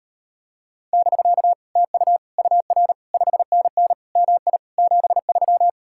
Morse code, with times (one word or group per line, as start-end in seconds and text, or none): 0.9-1.5 <BK>
1.8-2.2 TU
2.4-2.9 UR
3.1-3.9 5NN
4.2-4.6 MI
4.8-5.7 73